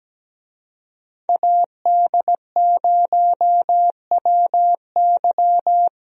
Morse code, heard as a keyed tone, about 17 wpm, 700 Hz. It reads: AD0WY
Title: USA Call Signs - 5 Characters - Repeated 3x - 17wpm